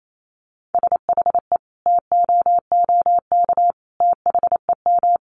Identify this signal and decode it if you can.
Morse code, heard as a keyed tone, about 28 wpm, 700 Hz. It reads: SHE TOOK THEM
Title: Sentences from Top 1300 Words 28wpm